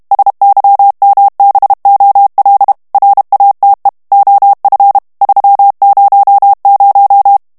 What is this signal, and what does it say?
Morse code, heard as a keyed tone, about 32 wpm, 800 Hz. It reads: SYMBOL RATE OF 300
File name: ANGLAIS texte clair 3 vitesse 30 mots minute